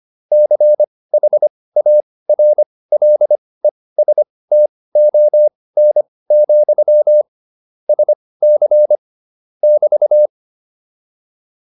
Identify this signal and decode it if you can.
Morse code, heard as a keyed tone, about 25 wpm, spaced as for 18 wpm, 600 Hz. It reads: CHARLESTON, SC <BT>
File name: QSO303 18